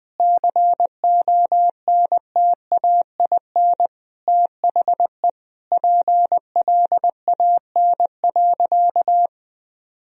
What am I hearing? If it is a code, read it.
Morse code, heard as a keyed tone, about 20 wpm, 700 Hz. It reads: CONTAIN THE PLAN.
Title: Sentences from Top 1000 Words - Review Part 2 - 20wpm